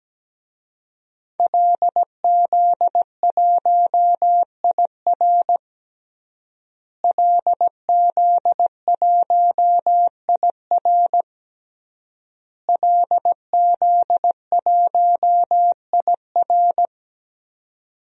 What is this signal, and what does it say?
Morse code, heard as a keyed tone, about 17 wpm, 700 Hz. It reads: LZ1IR LZ1IR LZ1IR
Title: DXCC-100 Call Signs - 5 Characters - Repeated 3x - 17wpm